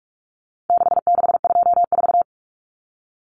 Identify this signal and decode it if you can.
Morse code, heard as a keyed tone, about 44 wpm, 700 Hz. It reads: 6624
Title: Numbers - 4-Digit 45wpm